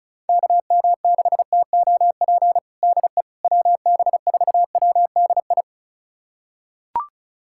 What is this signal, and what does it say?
Morse code, heard as a keyed tone, about 34 wpm, 700 Hz. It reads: KM6TOP DE WB4WDI
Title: USA Callsign DE Callsign - 6 Characters - 35wpm